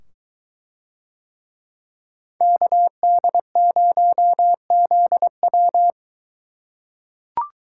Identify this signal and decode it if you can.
Morse code, heard as a keyed tone, about 22 wpm, 700 Hz. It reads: KD0ZW